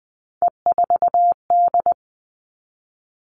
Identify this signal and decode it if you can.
Morse code, heard as a keyed tone, about 20 wpm, 700 Hz. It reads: E4D
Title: All Cut Numbers - 3-Digit 20wpm